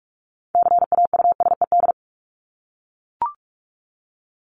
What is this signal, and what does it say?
Morse code, heard as a keyed tone, about 44 wpm, 700 Hz. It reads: CAUSED
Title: ED Suffix - Top 100 Words with Base Words 45wpm